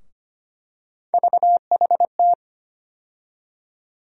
Morse code, heard as a keyed tone, about 25 wpm, 700 Hz. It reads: VHT
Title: USA Call Sign Postfixes - 3 Letters 25wpm